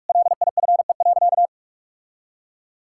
Morse code, heard as a keyed tone, about 46 wpm, 700 Hz. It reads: LIFE.
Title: Sentences from Top 1000 Words - Review Part 5 - 45wpm